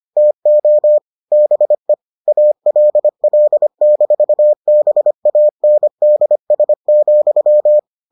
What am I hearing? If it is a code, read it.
Morse code, heard as a keyed tone, about 25 wpm, 600 Hz. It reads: TO BE ALL-BANDS,